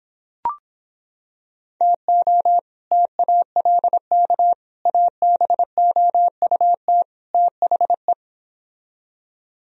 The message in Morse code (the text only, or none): TO TALK ABOUT THE